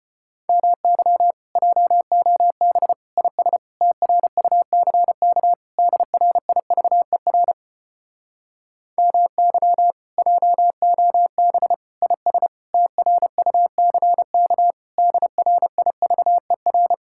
Morse code, heard as a keyed tone, about 33 wpm, 700 Hz. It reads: MY JOB IS TRUCK DRIVER MY JOB IS TRUCK DRIVER